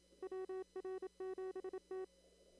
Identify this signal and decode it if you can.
Morse code, heard as a keyed tone, about 27 wpm, 380 Hz. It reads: WR7T
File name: CW / kt5j wpx cw-op-k5pi